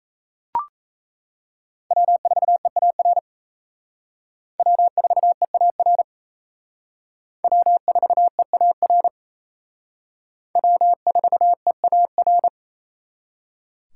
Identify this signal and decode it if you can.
Morse code, heard as a keyed tone, about 35 wpm, 700 Hz. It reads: W4EAR W4EAR W4EAR W4EAR
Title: Speed Racing - USA 1x3 Call Signs 28wpm